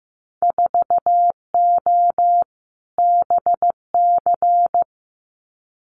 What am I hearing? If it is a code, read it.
Morse code, heard as a keyed tone, about 15 wpm, 700 Hz. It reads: 4O BC
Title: Winter Field Day Exchanges Encore 15wpm